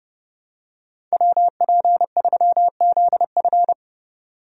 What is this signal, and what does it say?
Morse code, heard as a keyed tone, about 32 wpm, 700 Hz. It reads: WP3ZF
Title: USA Call Signs - 5 Characters - 30wpm